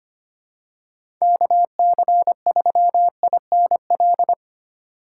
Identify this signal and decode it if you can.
Morse code, heard as a keyed tone, about 25 wpm, 700 Hz. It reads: KC3INL
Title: USA Call Signs - 8x Character Spacing - 25wpm